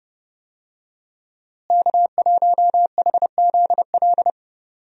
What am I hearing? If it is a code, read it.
Morse code, heard as a keyed tone, about 30 wpm, 700 Hz. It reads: K1HZL
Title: USA Call Signs - 5 Characters - Repeated 3x - 30wpm